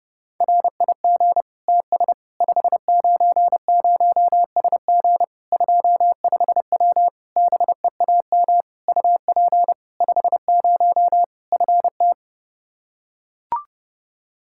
Morse code, heard as a keyed tone, about 30 wpm, 700 Hz. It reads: RIG TS 590SG 25W BEAM UP 50 FT
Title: Practice QSO - Exchange 2 - Info - 30wpm